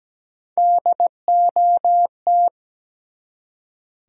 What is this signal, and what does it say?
Morse code, heard as a keyed tone, about 17 wpm, 700 Hz. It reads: DOT
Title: USA Common Names - Max 3 Letters - 17wpm